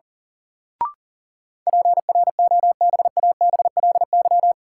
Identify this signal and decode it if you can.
Morse code, heard as a keyed tone, about 37 wpm, 700 Hz. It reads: PROBABLY
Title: Sentences from Top 700 Words - New Word Drill - 40wpm